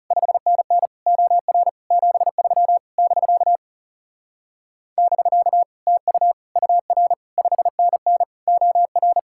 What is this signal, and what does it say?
Morse code, heard as a keyed tone, about 39 wpm, 700 Hz. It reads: 5NN OR 73 <BK> <BK> TU UR 5NN OR